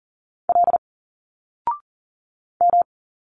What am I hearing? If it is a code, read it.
Morse code, heard as a keyed tone, about 39 wpm, 700 Hz. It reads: L M
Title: Single Letters 40wpm